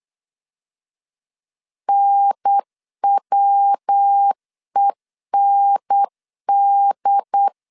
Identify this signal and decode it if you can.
Morse code, heard as a keyed tone, about 8 wpm, 800 Hz. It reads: NWEND
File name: CW / DET / 1 3 1500C S B 8wpm 4141